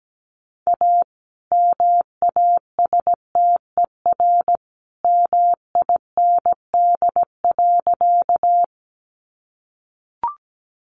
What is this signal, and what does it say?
Morse code, heard as a keyed tone, about 17 wpm, 700 Hz. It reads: A MASTER MIND.
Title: Sentences from Top 1300 Words 17wpm